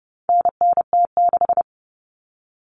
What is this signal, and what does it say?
Morse code, heard as a keyed tone, about 30 wpm, 700 Hz. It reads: NNT6
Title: Common Cut Numbers - 4-Digit 30wpm